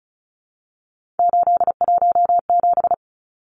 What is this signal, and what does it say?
Morse code, heard as a keyed tone, about 34 wpm, 700 Hz. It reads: 817
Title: Numbers - 3-Digit 35wpm